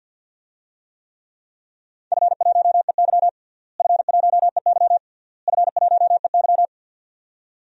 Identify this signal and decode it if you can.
Morse code, heard as a keyed tone, about 53 wpm, 700 Hz. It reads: F1EY F1EY F1EY
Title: DXCC-100 Call Signs - 4 Characters - Repeated 3x - 50wpm